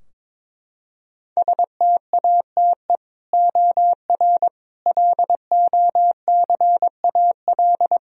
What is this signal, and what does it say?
Morse code, heard as a keyed tone, about 21 wpm, 700 Hz. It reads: STATE OR LOCAL
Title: Sentences from Top 1000 Words - Review Part 3 - 22wpm